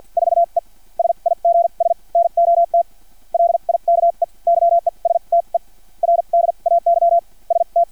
Morse code, heard as a keyed tone, about 50 wpm, spaced as for 36 wpm, 680 Hz. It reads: 4E SIGS NOT LIKE YESTERDAY ST
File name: archived / CRSnet